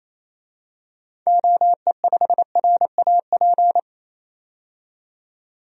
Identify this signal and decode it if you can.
Morse code, heard as a keyed tone, about 28 wpm, 700 Hz. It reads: OE5RAP
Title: DXCC-100 Call Signs - 6 Characters - 28wpm